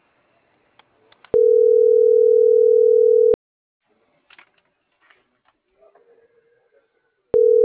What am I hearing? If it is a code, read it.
Ringback tone (North America), 440 + 480 Hz.